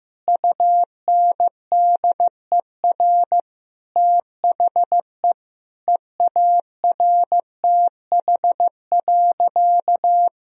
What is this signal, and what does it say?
Morse code, heard as a keyed tone, about 15 wpm, 700 Hz. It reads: UNDER THE EARTH.
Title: Sentences from Top 1000 Words - Review Part 1 - 15wpm